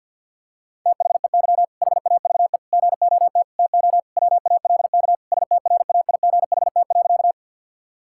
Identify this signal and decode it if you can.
Morse code, heard as a keyed tone, about 52 wpm, 700 Hz. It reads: THEY HAVE GOT TO WALK STRAIGHT.
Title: Sentences from Top 1000 Words - Review Part 6 - 50wpm